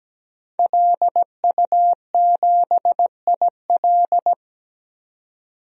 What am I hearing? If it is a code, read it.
Morse code, heard as a keyed tone, about 17 wpm, 700 Hz. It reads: LU7IL
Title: DXCC-100 Call Signs - 5 Characters - 17wpm